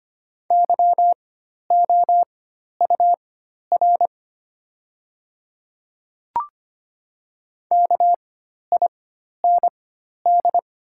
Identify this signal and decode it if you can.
Morse code, heard as a keyed tone, about 25 wpm, spaced as for 12 wpm, 700 Hz. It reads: YOUR KIND
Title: 4x Character Spacing - Top 200 Words - 25wpm